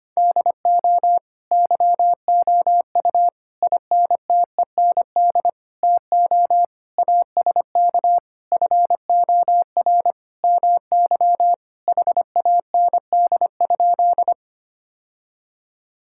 Morse code, heard as a keyed tone, about 25 wpm, 700 Hz. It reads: DO YOU INTEND TO ASK FOR MY HAND?
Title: Sentences from Top 1600 Words 25wpm